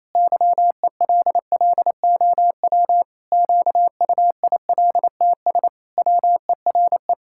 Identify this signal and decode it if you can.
Morse code, heard as a keyed tone, about 27 wpm, 700 Hz. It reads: YELLOW QUILTS WERE
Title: Pangram Challenge v2 - 28wpm